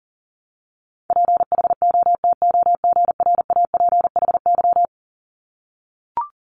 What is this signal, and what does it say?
Morse code, heard as a keyed tone, about 39 wpm, 700 Hz. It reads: PHOTOGRAPHY
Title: Multi-letter Phonemes 40wpm